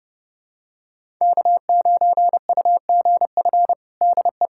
Morse code, heard as a keyed tone, about 32 wpm, 700 Hz. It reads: K9UGF DE
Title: USA Callsign DE Callsign - 5 Characters - 30wpm